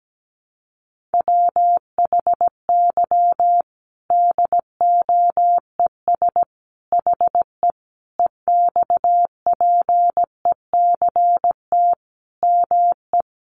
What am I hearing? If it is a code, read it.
Morse code, heard as a keyed tone, about 17 wpm, 700 Hz. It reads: WHY DOES HE EXPECT ME
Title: Sentences from Top 1100 Words 17wpm